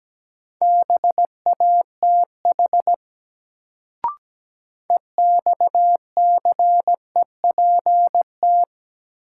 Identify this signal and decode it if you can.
Morse code, heard as a keyed tone, about 17 wpm, 700 Hz. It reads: BATH EXCEPT
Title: Sentences from Top 1600 Words - New Word Drill Alt - 17wpm